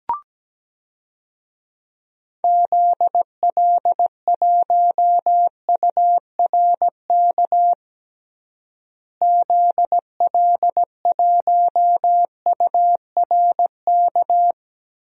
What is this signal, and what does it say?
Morse code, heard as a keyed tone, about 17 wpm, 700 Hz. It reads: ZL1URK ZL1URK
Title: DXCC-100 Call Signs - 6 Characters - Repeated 3x - 17wpm